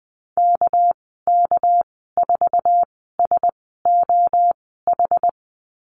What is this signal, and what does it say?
Morse code, heard as a keyed tone, about 20 wpm, spaced as for 15 wpm, 700 Hz. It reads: KK4SOH
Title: 213 Lesson 34g - Number 8 - Call Signs 20wpm